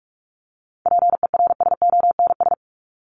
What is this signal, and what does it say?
Morse code, heard as a keyed tone, about 44 wpm, 700 Hz. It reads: PERSONS
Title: ES:S Suffix - Top 100 Words with Base Words 45wpm